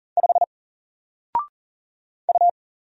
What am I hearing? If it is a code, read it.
Morse code, heard as a keyed tone, about 40 wpm, 700 Hz. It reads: 5 U
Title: Single Letter-Number Alt 40wpm